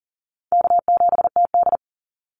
Morse code, heard as a keyed tone, about 40 wpm, 700 Hz. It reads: K7TD
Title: Easy Call Signs - 40wpm